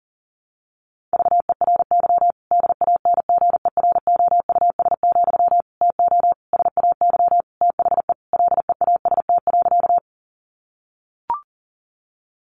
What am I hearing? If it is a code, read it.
Morse code, heard as a keyed tone, about 40 wpm, 700 Hz. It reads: VERY DANGEROUS, TO SAY THE LEAST.
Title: Sentences from Top 1200 Words 40wpm